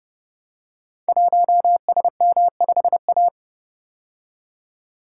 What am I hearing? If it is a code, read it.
Morse code, heard as a keyed tone, about 30 wpm, 700 Hz. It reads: 1SM5A
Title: Code Groups - 5 Letter:Numbers - 30wpm